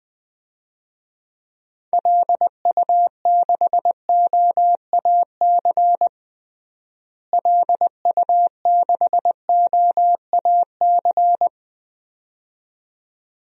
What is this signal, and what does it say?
Morse code, heard as a keyed tone, about 20 wpm, 700 Hz. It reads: LU6OAC LU6OAC
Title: DXCC-100 Call Signs - 6 Characters - Repeated 2x - 20wpm